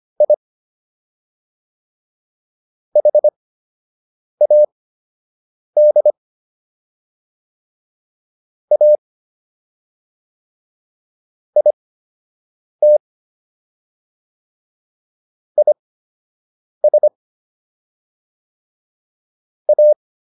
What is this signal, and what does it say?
Morse code, heard as a keyed tone, about 25 wpm, spaced as for 7 wpm, 600 Hz. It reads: I HAD A IT IS A